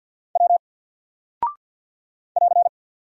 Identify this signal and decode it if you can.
Morse code, heard as a keyed tone, about 50 wpm, 700 Hz. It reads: W <AR>
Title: Single Character (Letter:Number:Punctuation:Prosign) 50wpm